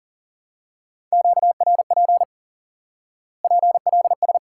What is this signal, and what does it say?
Morse code, heard as a keyed tone, about 40 wpm, 700 Hz. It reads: QRP PLS
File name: IQE Course 3 - Lesson 89 - YAGI - Two Elements - 40wpm